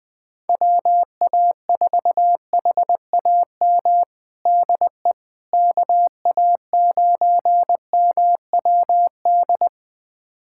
Morse code, heard as a keyed tone, about 20 wpm, 700 Hz. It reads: WA4HAM DE KA9MWD
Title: USA Callsign DE Callsign - 6 Characters - 20wpm